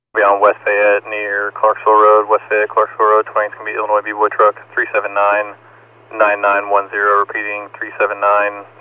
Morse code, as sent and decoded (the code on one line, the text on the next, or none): none